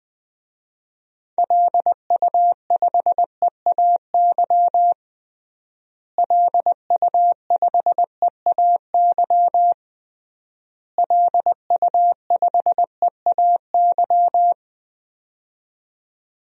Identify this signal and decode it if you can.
Morse code, heard as a keyed tone, about 20 wpm, 700 Hz. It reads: LU5EAY LU5EAY LU5EAY
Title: DXCC-100 Call Signs - 6 Characters - Repeated 3x - 20wpm